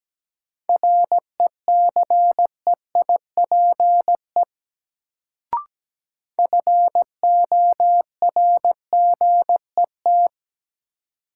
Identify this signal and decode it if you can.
Morse code, heard as a keyed tone, about 17 wpm, 700 Hz. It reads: RECEIPE FORGET
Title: Sentences from Top 1200 Words - New Word Drill - 17wpm